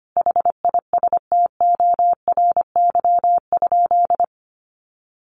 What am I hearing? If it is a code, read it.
Morse code, heard as a keyed tone, about 25 wpm, 700 Hz. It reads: HISTORY?
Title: Sentences from Top 600 Words 25wpm